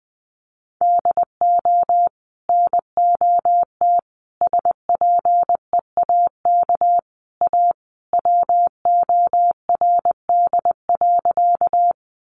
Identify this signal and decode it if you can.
Morse code, heard as a keyed tone, about 20 wpm, 700 Hz. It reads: DO NOT SPEAK A WORD.
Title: Sentences from Top 1100 Words 20wpm